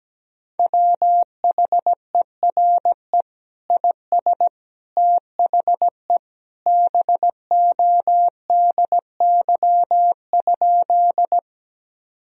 Morse code, heard as a keyed tone, about 17 wpm, 700 Hz. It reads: WHERE IS THE BODY?
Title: Sentences from Top 1000 Words - Review Part 2 - 17wpm